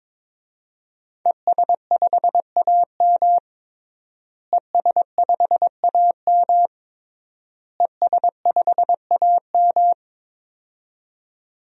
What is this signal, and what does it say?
Morse code, heard as a keyed tone, about 22 wpm, 700 Hz. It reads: ES5AM ES5AM ES5AM
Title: DXCC-100 Call Signs - 5 Characters - Repeated 3x - 22wpm